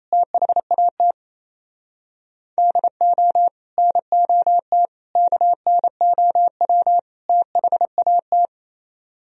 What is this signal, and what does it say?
Morse code, heard as a keyed tone, about 37 wpm, 700 Hz. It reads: THAT DO NOT KNOW THAT